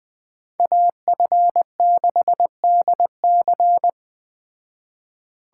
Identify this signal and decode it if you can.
Morse code, heard as a keyed tone, about 21 wpm, 700 Hz. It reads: AF6DC